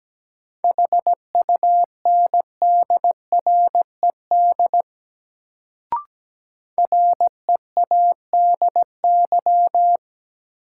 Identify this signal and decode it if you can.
Morse code, heard as a keyed tone, about 17 wpm, 700 Hz. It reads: HUNDRED READY
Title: Sentences from Top 400 Words - New Word Drill - 17wpm